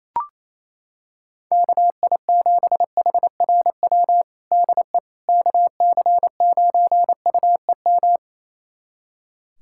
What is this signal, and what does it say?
Morse code, heard as a keyed tone, about 28 wpm, 700 Hz. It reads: KI7HRW DE KC9UEM
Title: USA Callsign DE Callsign - 6 Characters - 28wpm